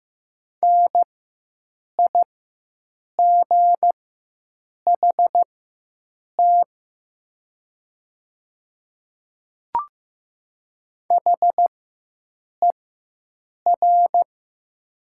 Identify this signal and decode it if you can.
Morse code, heard as a keyed tone, about 15 wpm, spaced as for 7 wpm, 700 Hz. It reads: NIGHT HER
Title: 4x Character Spacing - Top 300 Words - 15wpm